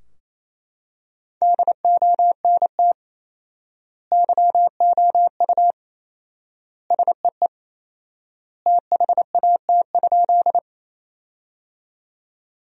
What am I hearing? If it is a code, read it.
Morse code, heard as a keyed tone, about 28 wpm, 700 Hz. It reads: DONT YOU SEE THAT?
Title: 4x Word-Spacing - Sentences from Top 300 Words - 28wpm